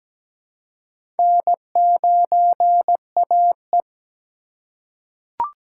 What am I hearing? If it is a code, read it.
Morse code, heard as a keyed tone, about 17 wpm, 700 Hz. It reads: N9AE
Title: USA Call Signs - 4 Characters - 3x Character Spacing - 17wpm